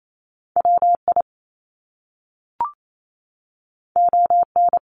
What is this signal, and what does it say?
Morse code, heard as a keyed tone, about 28 wpm, 700 Hz. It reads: WI ON
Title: Top Two Letter Combinations in Words 28wpm